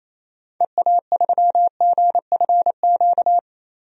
Morse code, heard as a keyed tone, about 28 wpm, 700 Hz. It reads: EA3GFQ
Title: DXCC-100 Call Signs - 6 Characters - 28wpm